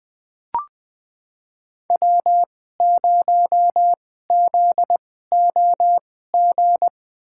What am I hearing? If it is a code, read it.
Morse code, heard as a keyed tone, about 20 wpm, spaced as for 14 wpm, 700 Hz. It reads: W0ZOG
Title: USA Call Signs - 5 Characters - 2x Character Spacing - 20wpm